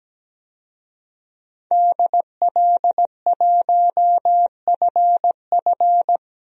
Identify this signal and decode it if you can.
Morse code, heard as a keyed tone, about 17 wpm, 700 Hz. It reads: DL1FF
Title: DXCC-100 Call Signs - 5 Characters - Repeated 2x - 17wpm